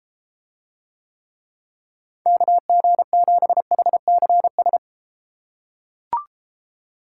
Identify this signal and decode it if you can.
Morse code, heard as a keyed tone, about 32 wpm, 700 Hz. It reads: KG7HCS